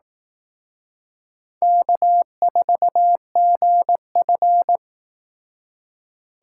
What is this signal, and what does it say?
Morse code, heard as a keyed tone, about 19 wpm, 700 Hz. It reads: K4GF